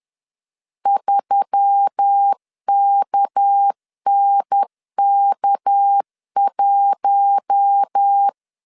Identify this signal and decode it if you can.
Morse code, heard as a keyed tone, about 10 wpm, 800 Hz. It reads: 3KNK1
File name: LEZIONI / 6 PKX0 / 6 PK 10wpm 2436